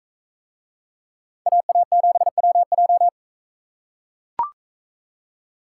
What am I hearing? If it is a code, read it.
Morse code, heard as a keyed tone, about 42 wpm, 700 Hz. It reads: AA7WJ